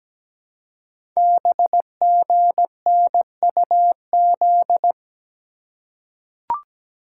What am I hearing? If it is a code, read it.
Morse code, heard as a keyed tone, about 17 wpm, 700 Hz. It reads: BGNUZ
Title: Code Groups - 5 Letters - 17wpm